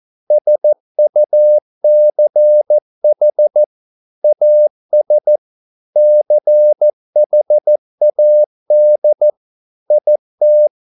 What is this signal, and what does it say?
Morse code, heard as a keyed tone, about 14 wpm, 600 Hz. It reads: SUCH AS CHAD IT